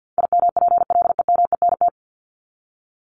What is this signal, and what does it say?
Morse code, heard as a keyed tone, about 50 wpm, 700 Hz. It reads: IMPLEMENT
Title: Top 1501-1750 Words 50wpm